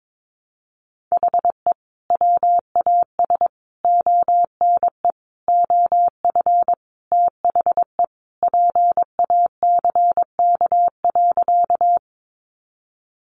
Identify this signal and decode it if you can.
Morse code, heard as a keyed tone, about 22 wpm, 700 Hz. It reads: HE WAS ONE OF THE PACK.
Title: Sentences from Top 1300 Words 22wpm